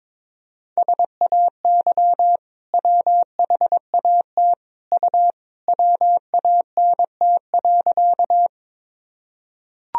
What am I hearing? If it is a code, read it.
Morse code, heard as a keyed tone, about 23 wpm, 700 Hz. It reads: SAY WHAT U WANT.
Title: Sentences from Top 200 Words and Abbreviations 22wpm